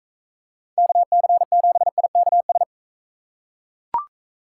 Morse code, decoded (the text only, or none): KC7IKS